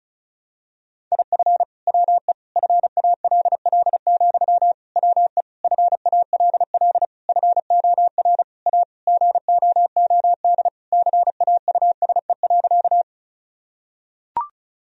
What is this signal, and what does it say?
Morse code, heard as a keyed tone, about 36 wpm, 700 Hz. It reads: IF WE FALL, WE FALL FOR A GOOD CAUSE.
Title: Sentences from Top 1000 Words - Review Part 2 - 35wpm